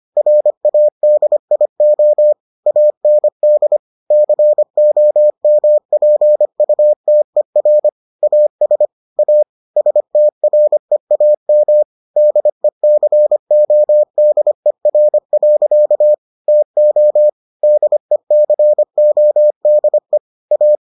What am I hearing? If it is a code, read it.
Morse code, heard as a keyed tone, about 25 wpm, 600 Hz. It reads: RADIO AND COMPUTER AS A STREAM DECODER. TO DECODE A